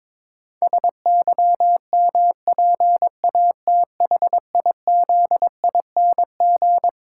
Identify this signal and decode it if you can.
Morse code, heard as a keyed tone, about 22 wpm, 700 Hz. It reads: SYMPATHIZING
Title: Pangram Challenge v2 - 22wpm